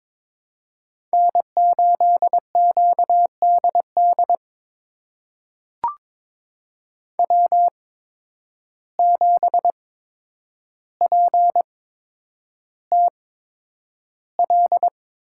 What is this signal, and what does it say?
Morse code, heard as a keyed tone, about 22 wpm, spaced as for 9 wpm, 700 Hz. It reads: N8QDD W7PTL
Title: USA Call Signs - 8x Character Spacing - 22wpm